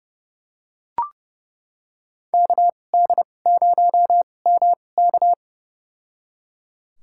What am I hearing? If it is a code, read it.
Morse code, heard as a keyed tone, about 31 wpm, spaced as for 22 wpm, 700 Hz. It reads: KD0MK
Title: USA Call Signs - 2x Character Spacing - 30wpm